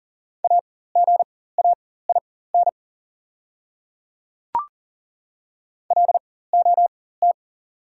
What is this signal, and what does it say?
Morse code, heard as a keyed tone, about 40 wpm, spaced as for 19 wpm, 690 Hz. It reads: AGAIN LOT
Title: 4x Character Spacing - Top 500 Words - 40wpm